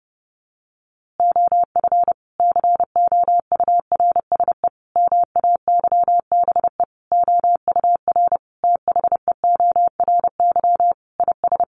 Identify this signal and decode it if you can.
Morse code, heard as a keyed tone, about 28 wpm, 700 Hz. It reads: OF COURSE MAYBE OUR THEORY IS
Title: A Sound Of Thunder 30wpm